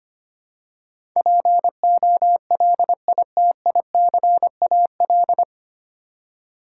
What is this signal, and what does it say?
Morse code, heard as a keyed tone, about 25 wpm, 700 Hz. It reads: POLITICAL
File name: Nine Letter Words - 25wpm